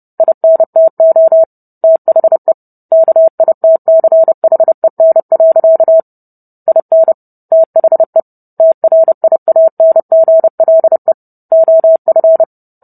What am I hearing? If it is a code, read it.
Morse code, heard as a keyed tone, about 30 wpm, 650 Hz. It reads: INTO THE KITCHEN. IN THE TRIANGLE OF